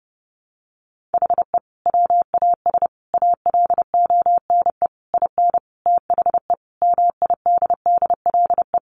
Morse code, heard as a keyed tone, about 30 wpm, 700 Hz. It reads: HE WAS ALONE IN THE MIDDLE
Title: Sentences from Top 1000 Words 30wpm